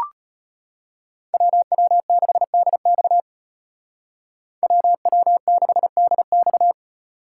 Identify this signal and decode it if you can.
Morse code, heard as a keyed tone, about 33 wpm, 700 Hz. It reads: WW6DX WW6DX